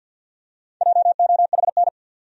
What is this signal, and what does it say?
Morse code, heard as a keyed tone, about 53 wpm, 700 Hz. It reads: JOHN
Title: USA Common Names - 50wpm